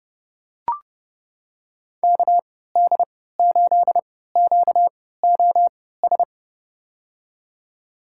Morse code, heard as a keyed tone, about 30 wpm, spaced as for 17 wpm, 700 Hz. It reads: KD8QOS
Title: USA Call Signs - 6 Characters - 3x Character Spacing - 30wpm